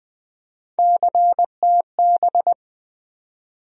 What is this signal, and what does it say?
Morse code, heard as a keyed tone, about 20 wpm, 700 Hz. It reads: CTB
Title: USA Call Sign Postfixes - 3 Letters 20wpm